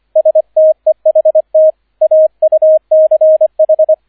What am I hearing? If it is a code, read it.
Morse code, heard as a keyed tone, about 25 wpm, 630 Hz, with radio noise. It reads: STEHT AUCH